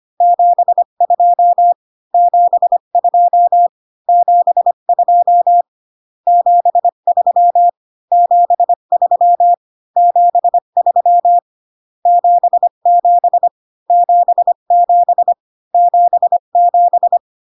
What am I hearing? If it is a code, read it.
Morse code, heard as a keyed tone, about 25 wpm, 700 Hz. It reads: 72 72 72 73 73 73 77 77 77